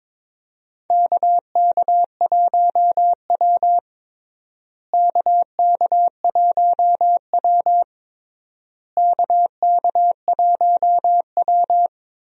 Morse code, heard as a keyed tone, about 22 wpm, 700 Hz. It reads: KK1W KK1W KK1W
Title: USA Call Signs - 4 Characters - Repeated 3x - 22wpm